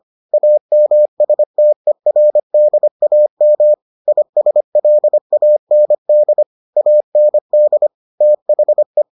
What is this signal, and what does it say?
Morse code, heard as a keyed tone, about 25 wpm, 600 Hz. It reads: AMSTERDAM ISLAND AND THE